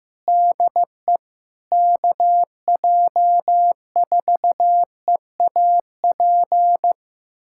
Morse code, heard as a keyed tone, about 15 wpm, 700 Hz. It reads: DE KJ4EAP
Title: USA Callsign DE Callsign 15wpm